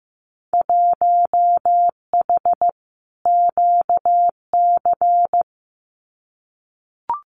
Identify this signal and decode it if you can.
Morse code, heard as a keyed tone, about 15 wpm, 700 Hz. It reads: 1H QC
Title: Winter Field Day Exchanges 15wpm